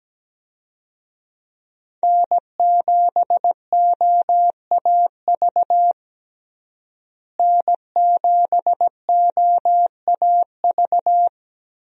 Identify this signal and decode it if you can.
Morse code, heard as a keyed tone, about 17 wpm, 700 Hz. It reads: N7OAV N7OAV
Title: USA Call Signs - 5 Characters - Repeated 2x - 17wpm